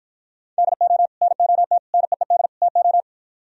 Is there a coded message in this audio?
Yes: Morse code from 0.6 to 3.0 s, heard as a keyed tone, about 52 wpm, 700 Hz. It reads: DO NOT NEED TO